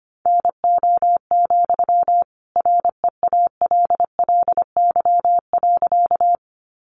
Morse code, heard as a keyed tone, about 25 wpm, 700 Hz. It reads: NO, REALLY.
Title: Sentences from Top 600 Words 25wpm